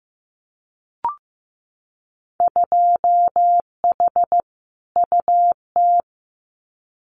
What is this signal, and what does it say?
Morse code, heard as a keyed tone, about 15 wpm, 700 Hz. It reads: 2H UT
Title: Winter Field Day Exchanges Encore 15wpm